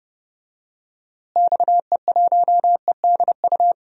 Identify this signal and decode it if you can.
Morse code, heard as a keyed tone, about 30 wpm, 700 Hz. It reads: XE1EDU
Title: DXCC-100 Call Signs - 6 Characters - Repeated 3x - 30wpm